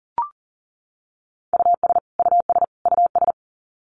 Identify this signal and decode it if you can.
Morse code, heard as a keyed tone, about 40 wpm, 700 Hz. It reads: US US US
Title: Top 200 Words Repeated 40wpm